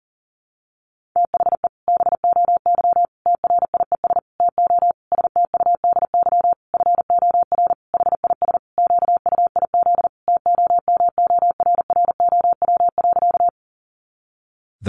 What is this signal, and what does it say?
Morse code, heard as a keyed tone, about 40 wpm, 700 Hz. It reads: THE BOY TRIES TO STUDY FOR HIS QUIZ TOMORROW.
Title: Additional Easy Phrases 40wpm